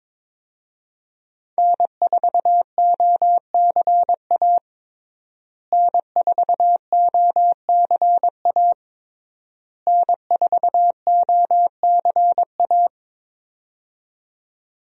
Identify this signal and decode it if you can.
Morse code, heard as a keyed tone, about 21 wpm, 700 Hz. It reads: N4OCA N4OCA N4OCA
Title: USA Call Signs - 5 Characters - Repeated 3x - 22wpm